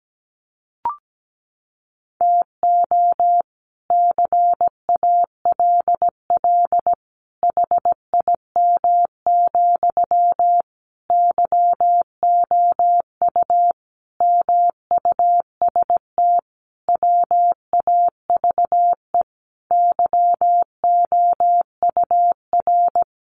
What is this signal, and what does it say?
Morse code, heard as a keyed tone, about 17 wpm, 700 Hz. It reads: TO CALL HIM, YOU MUST WAVE YOUR
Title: Sentences from Top 1400 Words 17wpm